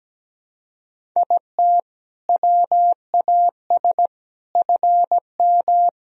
Morse code, heard as a keyed tone, about 17 wpm, 700 Hz. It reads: IT WAS FM
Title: Sentences from Top 100 Words and Abbreviations 17wpm